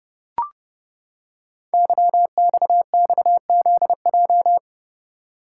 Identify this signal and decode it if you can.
Morse code, heard as a keyed tone, about 30 wpm, 700 Hz. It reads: YXXZJ
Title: Code Groups - 5 Letters - 30wpm